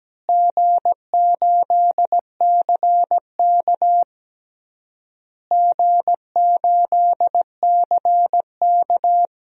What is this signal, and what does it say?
Morse code, heard as a keyed tone, about 17 wpm, 700 Hz. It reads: G8CK G8CK
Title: DXCC-100 Call Signs - 4 Characters - Repeated 3x - 17wpm